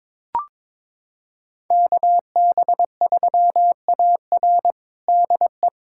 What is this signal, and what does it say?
Morse code, heard as a keyed tone, about 22 wpm, 700 Hz. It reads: KB3AR DE
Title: USA Callsign DE Callsign - 5 Characters - 22wpm